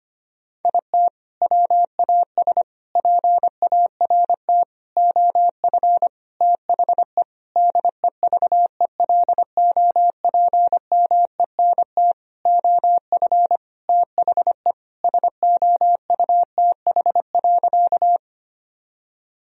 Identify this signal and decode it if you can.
Morse code, heard as a keyed tone, about 25 wpm, 700 Hz. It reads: IT WAS PART OF THE DEVELOPMENT OF THE SOUTH.
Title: Sentences from Top 1000 Words - Review Part 4 - 25wpm